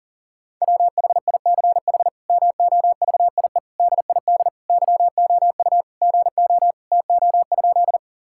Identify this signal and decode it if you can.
Morse code, heard as a keyed tone, about 37 wpm, 700 Hz. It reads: WHICH MOVIE DID YOU GO TO?
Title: Sentences from Top 1000 Words - Review Part 5 - 40wpm